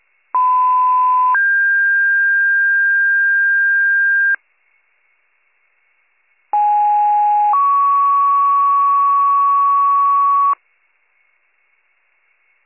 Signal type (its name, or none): two-tone page